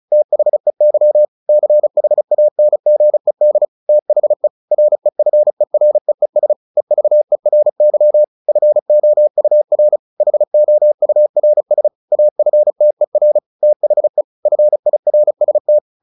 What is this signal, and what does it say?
Morse code, heard as a keyed tone, about 34 wpm, 600 Hz. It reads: THEY CHANGED THE REFEREES EVERY FOUR HOURS AFTER THE FIRST
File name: the old man and the sea 2 35wpm